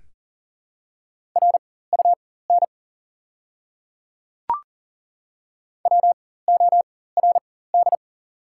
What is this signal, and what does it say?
Morse code, heard as a keyed tone, about 40 wpm, spaced as for 19 wpm, 700 Hz. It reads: RUN WORD
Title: 4x Character Spacing - Top 500 Words - 40wpm